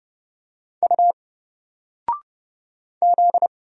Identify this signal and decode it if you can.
Morse code, heard as a keyed tone, about 30 wpm, 700 Hz. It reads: U Z
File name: Single-Letter-Number-30wpm